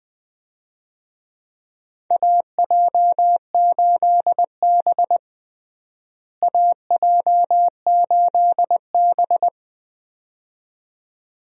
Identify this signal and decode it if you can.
Morse code, heard as a keyed tone, about 20 wpm, 700 Hz. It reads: AJ8B AJ8B
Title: USA Call Signs - 4 Characters - Repeated 2x - 20wpm